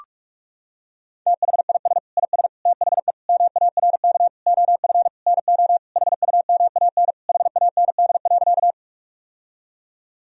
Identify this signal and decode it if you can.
Morse code, heard as a keyed tone, about 46 wpm, 700 Hz. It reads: THIS IS THE MARK OF NO HUMAN HAND.
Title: Sentences from Top 1000 Words - Review Part 3 - 45wpm